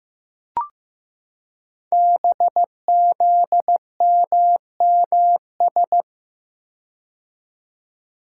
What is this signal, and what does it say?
Morse code, heard as a keyed tone, about 15 wpm, 700 Hz. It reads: BZMMS
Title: Code Groups - 5 Letters - 15wpm